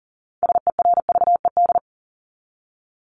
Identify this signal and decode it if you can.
Morse code, heard as a keyed tone, about 40 wpm, 700 Hz. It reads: SERVED
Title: ED Suffix - Top 100 Words with Base Words 40wpm